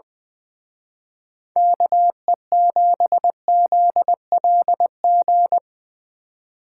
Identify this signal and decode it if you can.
Morse code, heard as a keyed tone, about 23 wpm, 700 Hz. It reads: KE7ZLG
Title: Speed Racing - USA Call Signs 20wpm